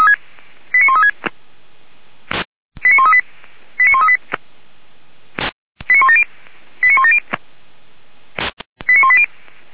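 Five-tone selective call (ZVEI), heard as quick runs of five tones, tones of about 70 ms. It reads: 89136 89138 89138 89170 89170 89180
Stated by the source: Five-tone ZVEI selective call 89135, received by SDR for feuerwehr-orlamuende (F85360).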